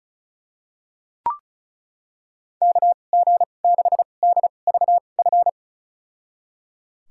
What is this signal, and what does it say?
Morse code, heard as a keyed tone, about 34 wpm, spaced as for 25 wpm, 700 Hz. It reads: KG6DVF
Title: USA Call Signs - 6 Characters - 2x Character Spacing - 35wpm